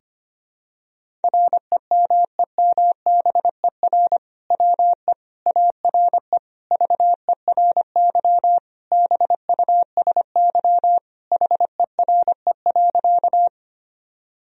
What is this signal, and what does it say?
Morse code, heard as a keyed tone, about 25 wpm, 700 Hz. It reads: REMEMBER WE ARE VERY BUSY HERE.
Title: Sentences from Top 1000 Words - Review Part 3 - 25wpm